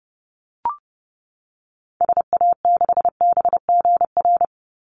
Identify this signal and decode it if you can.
Morse code, heard as a keyed tone, about 30 wpm, 700 Hz. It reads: SA6BGR
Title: Call Signs - 30wpm